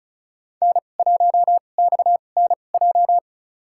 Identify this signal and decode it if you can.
Morse code, heard as a keyed tone, about 36 wpm, spaced as for 25 wpm, 690 Hz. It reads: N1XNJ
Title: USA Call Signs - 5 Characters - 2x Character Spacing - 35wpm